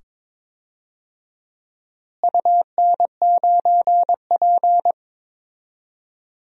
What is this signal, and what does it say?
Morse code, heard as a keyed tone, about 21 wpm, 700 Hz. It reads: UN9P
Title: DXCC-100 Call Signs - 4 Characters - Repeated 2x - 22wpm